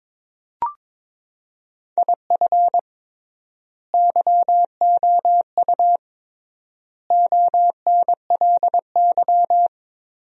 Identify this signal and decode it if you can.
Morse code, heard as a keyed tone, about 21 wpm, 700 Hz. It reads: IF YOU ONLY
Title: 3x Word-Spacing - Sentences from Top 400 Words - 22wpm